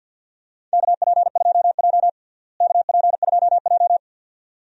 Morse code, heard as a keyed tone, about 52 wpm, 700 Hz. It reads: KP2J KP2J
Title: DXCC-100 Call Signs - 4 Characters - Repeated 2x - 50wpm